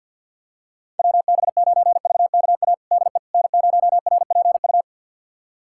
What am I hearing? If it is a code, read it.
Morse code, heard as a keyed tone, about 49 wpm, 700 Hz. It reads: WB9VKA DE N0RPU
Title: USA Callsign DE Callsign 50wpm